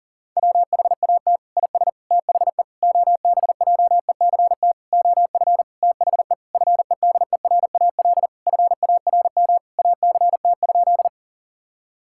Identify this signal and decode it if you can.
Morse code, heard as a keyed tone, about 37 wpm, 700 Hz. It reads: WHAT IS THE OBJECT OF THE FEDERAL FARM ACT?
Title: Sentences from Top 1000 Words - Review Part 5 - 40wpm